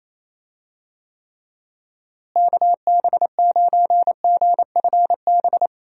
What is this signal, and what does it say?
Morse code, heard as a keyed tone, about 27 wpm, 700 Hz. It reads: KB9GFB
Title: USA Call Signs - 6 Characters - Repeated 2x - 28wpm